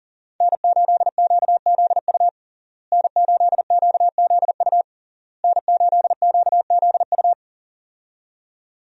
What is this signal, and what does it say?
Morse code, heard as a keyed tone, about 37 wpm, 700 Hz. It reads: N8QZU N8QZU N8QZU
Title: USA Call Signs - 5 Characters - Repeated 3x - 40wpm